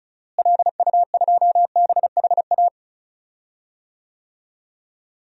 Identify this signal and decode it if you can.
Morse code, heard as a keyed tone, about 36 wpm, 700 Hz. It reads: LU2BHA
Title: DXCC-100 Call Signs - 6 Characters - Repeated 2x - 35wpm